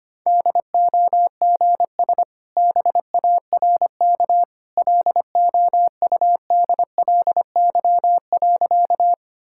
Morse code, heard as a keyed tone, about 25 wpm, 690 Hz. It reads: DOGS BARK LOUDLY.
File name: Easy Sentences with Hints - 25wpm